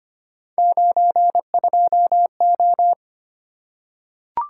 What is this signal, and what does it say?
Morse code, heard as a keyed tone, about 25 wpm, 700 Hz. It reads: 92O
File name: Three Letter-Number Combinations - 25wpm